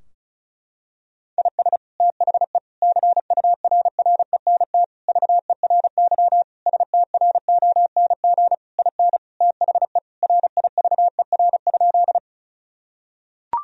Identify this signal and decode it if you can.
Morse code, heard as a keyed tone, about 36 wpm, 700 Hz. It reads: IS THE CURRENT VERY STRONG IN THE RIVER?
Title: Sentences from Top 1000 Words - Review Part 4 - 35wpm